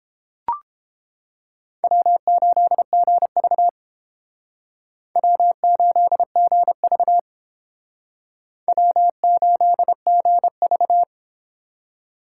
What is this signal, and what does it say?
Morse code, heard as a keyed tone, about 29 wpm, 700 Hz. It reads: W8GV W8GV W8GV